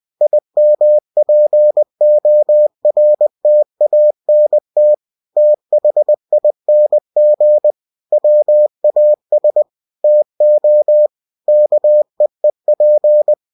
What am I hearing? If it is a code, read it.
Morse code, heard as a keyed tone, about 20 wpm, 600 Hz. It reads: IMPORTANT THING WAS TO KEEP
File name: winnie-20wpm